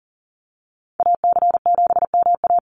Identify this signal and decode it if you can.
Morse code, heard as a keyed tone, about 39 wpm, 700 Hz. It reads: AC7MA
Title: NA SOTA Activator Callsigns 40wpm